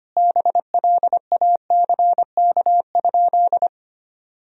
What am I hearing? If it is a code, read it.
Morse code, heard as a keyed tone, about 25 wpm, 700 Hz. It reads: BLACK?
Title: Sentences from Top 1000 Words - Review Part 2 - 25wpm